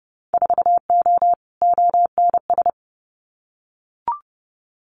Morse code, heard as a keyed tone, about 30 wpm, 700 Hz. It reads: 4O ONS
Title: Winter Field Day Exchanges 30wpm